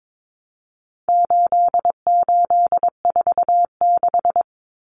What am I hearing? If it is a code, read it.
Morse code, heard as a keyed tone, about 22 wpm, 700 Hz. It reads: 8846
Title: Numbers - 4-Digit 22wpm